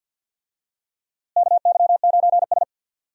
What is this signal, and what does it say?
Morse code, heard as a keyed tone, about 47 wpm, 700 Hz. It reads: KY9S
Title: USA Call Signs - 4 Characters - Repeated 2x - 50wpm